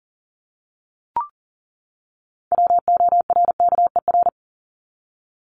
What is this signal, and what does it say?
Morse code, heard as a keyed tone, about 40 wpm, 700 Hz. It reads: WORKER
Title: ES:S Suffix - Top 100 Words with Base Words 40wpm